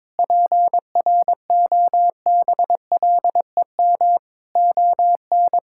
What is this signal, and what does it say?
Morse code, heard as a keyed tone, about 21 wpm, 700 Hz. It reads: PROBLEM ON
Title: Sentences from Top 1000 Words - Review Part 6 - 22wpm